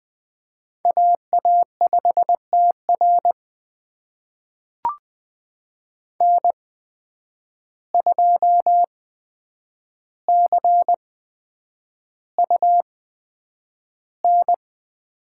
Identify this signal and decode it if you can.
Morse code, heard as a keyed tone, about 20 wpm, spaced as for 9 wpm, 700 Hz. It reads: AA5TR N2CUN
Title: USA Call Signs - 5 Characters - 8x Character Spacing - 20wpm